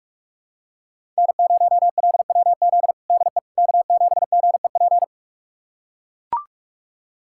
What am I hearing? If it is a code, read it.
Morse code, heard as a keyed tone, about 43 wpm, 700 Hz. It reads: N0LWZ DE K7GEP
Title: USA Callsign DE Callsign - 5 Characters - 45wpm